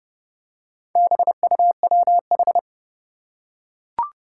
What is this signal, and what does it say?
Morse code, heard as a keyed tone, about 30 wpm, 700 Hz. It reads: BUWH
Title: Four Challenging Letter-Number Combinations 30wpm